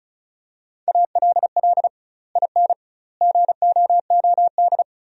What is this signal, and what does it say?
Morse code, heard as a keyed tone, about 34 wpm, 700 Hz. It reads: ALL IN GOOD
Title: 2x Word-Spacing - Sentences from Top 200 Words - 35wpm